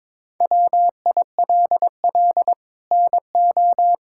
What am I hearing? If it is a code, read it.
Morse code, heard as a keyed tone, about 22 wpm, 700 Hz. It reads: WILL NO
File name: practice / 22wpm / Sets of 6 Words - 22wpm